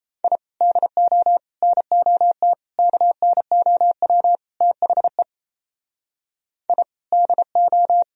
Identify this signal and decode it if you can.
Morse code, heard as a keyed tone, about 37 wpm, 700 Hz. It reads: I DO NOT KNOW THE I DO